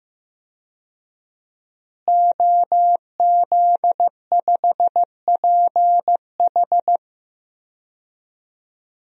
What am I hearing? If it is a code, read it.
Morse code, heard as a keyed tone, about 15 wpm, 700 Hz. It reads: OZ5PH
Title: DXCC-100 Call Signs - 5 Characters - Repeated 3x - 15wpm